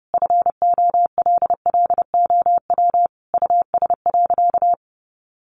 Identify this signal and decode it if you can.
Morse code, heard as a keyed tone, about 30 wpm, 700 Hz. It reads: FOLLOW US.
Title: Sentences from Top 1500 Words 30wpm